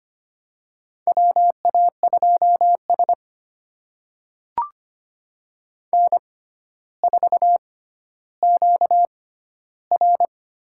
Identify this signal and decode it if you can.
Morse code, heard as a keyed tone, about 25 wpm, spaced as for 13 wpm, 700 Hz. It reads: WA2S N4QR
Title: USA Call Signs - 4 Characters - 6x Character Spacing - 25wpm